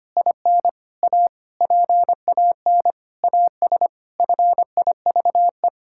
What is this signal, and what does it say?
Morse code, heard as a keyed tone, about 25 wpm, 700 Hz. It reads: IN A PAN AS FIVE
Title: Pangram Challenge v2 - 25wpm